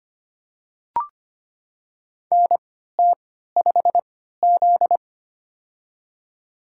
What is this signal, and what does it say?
Morse code, heard as a keyed tone, about 25 wpm, spaced as for 14 wpm, 700 Hz. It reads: NT5Z
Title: USA Call Signs - 4 Characters - 3x Character Spacing - 25wpm